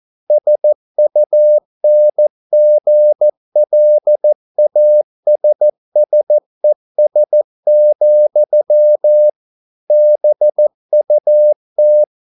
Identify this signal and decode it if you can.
Morse code, heard as a keyed tone, about 14 wpm, 600 Hz. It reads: SUNGLASSES, BUT